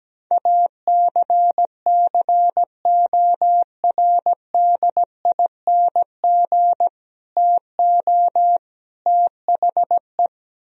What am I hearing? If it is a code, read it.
Morse code, heard as a keyed tone, about 17 wpm, 700 Hz. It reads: ACCORDING TO THE
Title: Speed-Racing - Sets of 3 Words 17wpm